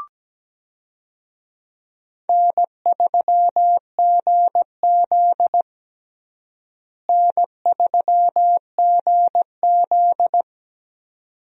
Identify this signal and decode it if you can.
Morse code, heard as a keyed tone, about 17 wpm, 700 Hz. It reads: N3GZ N3GZ
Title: USA Call Signs - 4 Characters - Repeated 2x - 17wpm